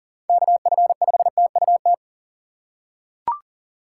Morse code, heard as a keyed tone, about 40 wpm, 700 Hz. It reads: KF5TUT
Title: USA Call Signs - 6 Characters - 2x Character Spacing - 40wpm